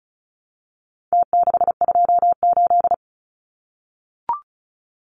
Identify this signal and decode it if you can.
Morse code, heard as a keyed tone, about 34 wpm, 700 Hz. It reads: T628
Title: Numbers - Contest Style - 4-Digit 35wpm